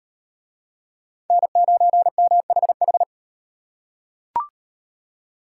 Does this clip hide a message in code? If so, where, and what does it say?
Yes: Morse code from 1.3 to 3.0 s, heard as a keyed tone, about 38 wpm, 700 Hz. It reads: N9MHH